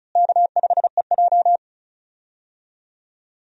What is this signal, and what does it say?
Morse code, heard as a keyed tone, about 36 wpm, 700 Hz. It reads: K5EJ
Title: USA Call Signs - 4 Characters - Repeated 3x - 35wpm